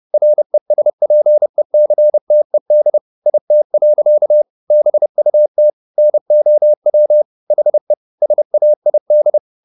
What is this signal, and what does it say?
Morse code, heard as a keyed tone, about 30 wpm, 600 Hz. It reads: RESPECTED IT. BUT NOW HE SAID
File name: the old man and the sea / the old man and the sea 1 30wpm